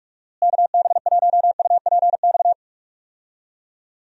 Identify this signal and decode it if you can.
Morse code, heard as a keyed tone, about 41 wpm, 700 Hz. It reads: KB1UPX
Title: USA Call Signs - 6 Characters - Repeated 3x - 45wpm